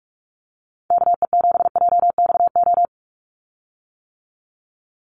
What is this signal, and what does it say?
Morse code, heard as a keyed tone, about 44 wpm, 700 Hz. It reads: KE7JXO
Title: USA Call Signs 45wpm